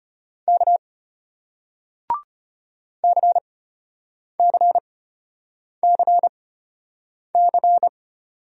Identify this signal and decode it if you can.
Morse code, heard as a keyed tone, about 34 wpm, spaced as for 8 wpm, 700 Hz. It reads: K CCCC